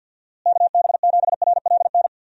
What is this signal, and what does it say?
Morse code, heard as a keyed tone, about 46 wpm, 700 Hz. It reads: KB7RLN
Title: USA Call Signs - 6 Characters - Repeated 2x - 50wpm